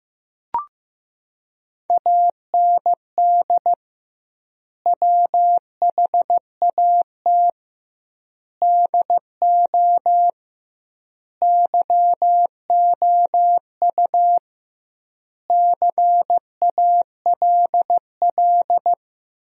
Morse code, heard as a keyed tone, about 15 wpm, 700 Hz. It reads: AND WHAT DO YOU CALL
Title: 2x Word-Spacing - Sentences from Top 100 Words - 15wpm